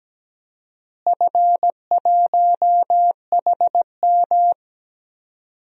Morse code, heard as a keyed tone, about 17 wpm, 700 Hz. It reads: F1HM
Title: DXCC-100 Call Signs - 4 Characters - Repeated 2x - 17wpm